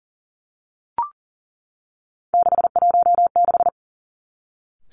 Morse code, heard as a keyed tone, about 42 wpm, 700 Hz. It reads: 616
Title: Three Challenging Letter-Number Combinations 40wpm